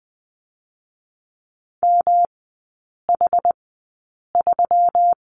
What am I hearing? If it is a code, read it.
Morse code, heard as a keyed tone, about 20 wpm, spaced as for 8 wpm, 700 Hz. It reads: MH3
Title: 184 Lesson 30e - Letter b - Three Characters 20wpm